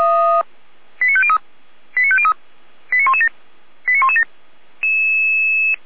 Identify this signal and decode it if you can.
Five-tone selective call (ZVEI), heard as quick runs of five tones, tones of about 70 ms. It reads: 89503 89503 89117 89117